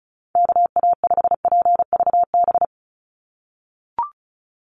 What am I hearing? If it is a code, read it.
Morse code, heard as a keyed tone, about 34 wpm, 700 Hz. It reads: KA5PVB
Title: NA SOTA Chaser Callsigns 35wpm